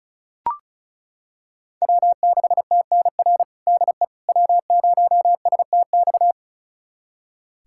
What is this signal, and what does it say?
Morse code, heard as a keyed tone, about 36 wpm, 700 Hz. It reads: W6TNR DE W0STX
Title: USA Callsign DE Callsign - 5 Characters - 35wpm